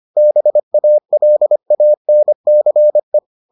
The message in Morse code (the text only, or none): BALANCE